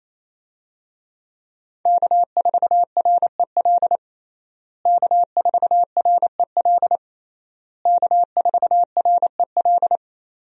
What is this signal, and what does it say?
Morse code, heard as a keyed tone, about 28 wpm, 700 Hz. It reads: K4REL K4REL K4REL
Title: USA Call Signs - 5 Characters - Repeated 3x - 28wpm